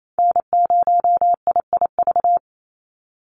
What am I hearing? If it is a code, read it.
Morse code, heard as a keyed tone, about 28 wpm, 700 Hz. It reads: N0IIV
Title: USA Call Signs 28wpm